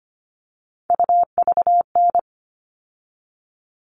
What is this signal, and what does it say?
Morse code, heard as a keyed tone, about 25 wpm, 700 Hz. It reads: UVN
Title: All Cut Numbers - 3-Digit 25wpm